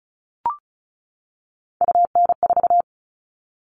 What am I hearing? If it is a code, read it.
Morse code, heard as a keyed tone, about 34 wpm, 700 Hz. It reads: UN4
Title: All Cut Numbers - 3-Digit 35wpm